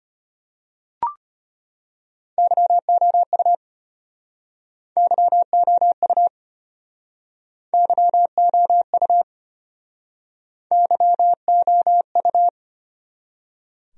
Morse code, heard as a keyed tone, about 32 wpm, 700 Hz. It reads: YOU YOU YOU YOU